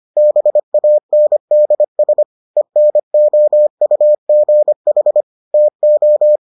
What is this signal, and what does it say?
Morse code, heard as a keyed tone, about 25 wpm, 600 Hz. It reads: BANDS ENOUGH TO